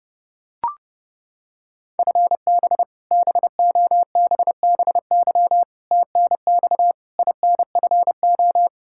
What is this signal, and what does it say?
Morse code, heard as a keyed tone, about 30 wpm, 700 Hz. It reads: FB BOBBY TNX INFO
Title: Practice QSO - Exchange 3 - More Info - 30wpm